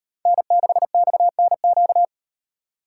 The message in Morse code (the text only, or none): N6XNQ